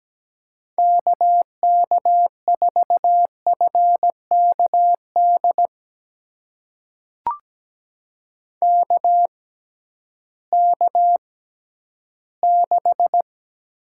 Morse code, heard as a keyed tone, about 17 wpm, spaced as for 11 wpm, 700 Hz. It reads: KK4FKD KK6
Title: USA Call Signs - 6 Characters - 6x Character Spacing - 17wpm